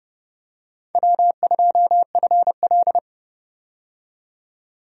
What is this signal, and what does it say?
Morse code, heard as a keyed tone, about 30 wpm, 700 Hz. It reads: W2FL
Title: USA Call Signs - 4 Characters - 30wpm